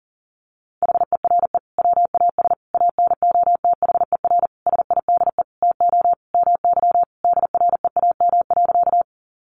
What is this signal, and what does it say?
Morse code, heard as a keyed tone, about 40 wpm, 700 Hz. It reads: HERE WAS ANOTHER SIDE TO MY DREAM.
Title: Sentences from Top 1500 Words 40wpm